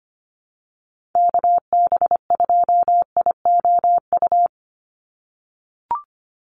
Morse code, heard as a keyed tone, about 25 wpm, 700 Hz. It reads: KB2IOU
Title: USA Call Signs 25wpm